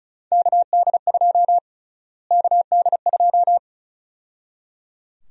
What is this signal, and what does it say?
Morse code, heard as a keyed tone, about 34 wpm, 700 Hz. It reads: KD2 KD2
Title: USA Call Sign Prefixes - Repeated 2x - 35wpm